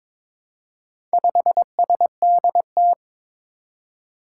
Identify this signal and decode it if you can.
Morse code, heard as a keyed tone, about 22 wpm, 700 Hz. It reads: 5SDT
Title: Four Challenging Letter-Number Combinations 22wpm